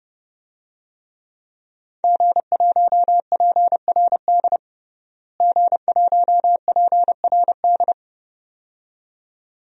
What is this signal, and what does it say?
Morse code, heard as a keyed tone, about 30 wpm, 700 Hz. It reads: G1PRD G1PRD
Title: DXCC-100 Call Signs - 5 Characters - Repeated 2x - 30wpm